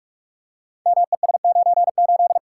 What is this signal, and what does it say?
Morse code, heard as a keyed tone, about 44 wpm, 700 Hz. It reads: MES98
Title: Code Groups - 5 Letter:Numbers - 45wpm